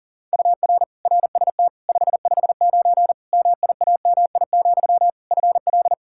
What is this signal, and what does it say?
Morse code, heard as a keyed tone, about 37 wpm, 700 Hz. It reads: UR RST 559 MIAMI, FL
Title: K1USN SST QSO Practice 40wpm